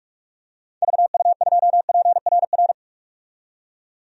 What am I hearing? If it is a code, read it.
Morse code, heard as a keyed tone, about 46 wpm, 700 Hz. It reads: VU2PRR
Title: DXCC-100 Call Signs - 6 Characters - 45wpm